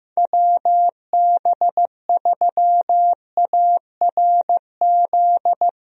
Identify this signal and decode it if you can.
Morse code, heard as a keyed tone, about 15 wpm, 700 Hz. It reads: WB3ARZ
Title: USA Call Signs - 6 Characters - 15wpm